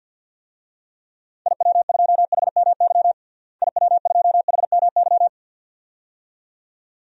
Morse code, heard as a keyed tone, about 52 wpm, 700 Hz. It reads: IW2HMY IW2HMY
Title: DXCC-100 Call Signs - 6 Characters - Repeated 2x - 50wpm